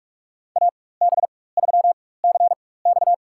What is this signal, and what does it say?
Morse code, heard as a keyed tone, about 45 wpm, spaced as for 21 wpm, 700 Hz. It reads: AB3CX
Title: USA Call Signs - 4x Character Spacing - 45wpm